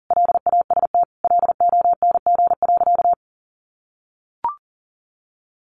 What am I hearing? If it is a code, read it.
Morse code, heard as a keyed tone, about 40 wpm, 700 Hz. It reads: LAST LONG.
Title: Sentences from Top 1300 Words 40wpm